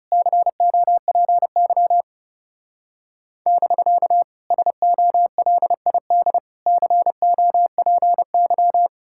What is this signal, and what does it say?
Morse code, heard as a keyed tone, about 37 wpm, 700 Hz. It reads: COPY <BK> SOLID COPY
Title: Speed-Racing Common QSO Phrases - 30wpm